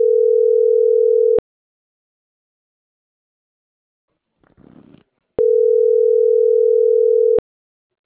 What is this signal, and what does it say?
Ringback tone (North America), 440 + 480 Hz.